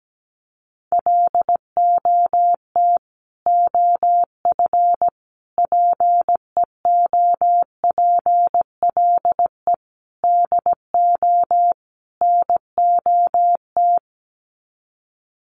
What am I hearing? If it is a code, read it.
Morse code, heard as a keyed tone, about 17 wpm, 700 Hz. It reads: LOT OF PEOPLE DO NOT
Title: Sets of 5 Words Encore 17wpm